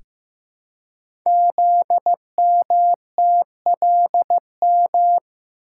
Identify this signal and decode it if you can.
Morse code, heard as a keyed tone, about 15 wpm, 700 Hz. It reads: ZMTLM
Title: Code Groups - 5 Letters - 15wpm